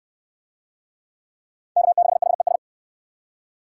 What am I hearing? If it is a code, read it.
Morse code, heard as a keyed tone, about 59 wpm, 700 Hz. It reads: K6HES